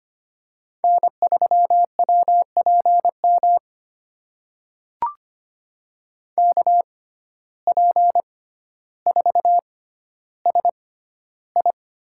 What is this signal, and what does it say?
Morse code, heard as a keyed tone, about 25 wpm, spaced as for 13 wpm, 700 Hz. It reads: N3WPM KP4SI
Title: USA Call Signs - 5 Characters - 6x Character Spacing - 25wpm